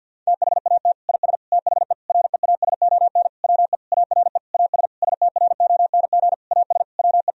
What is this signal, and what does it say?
Morse code, heard as a keyed tone, about 50 wpm, 700 Hz. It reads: THAT IS THE REASON WE ARE AS STRONG AS WE
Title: Sentences from Top 1000 Words - Review Part 4 - 50wpm